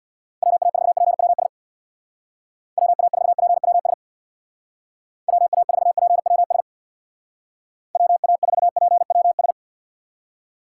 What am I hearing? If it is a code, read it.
Morse code, heard as a keyed tone, about 66 wpm, 700 Hz. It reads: WA4PWS WA4PWS WA4PWS WA4PWS